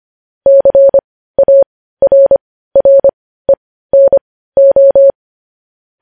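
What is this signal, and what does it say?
Morse code, heard as a keyed tone, about 25 wpm, spaced as for 15 wpm, 550 Hz. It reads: CARRENO
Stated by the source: Morse code news headlines sent at 15 wpm.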